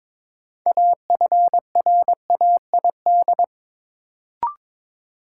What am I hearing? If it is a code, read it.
Morse code, heard as a keyed tone, about 22 wpm, 700 Hz. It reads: AFRAID
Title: Sentences from Top 1500 Words - New Word Drill - 22wpm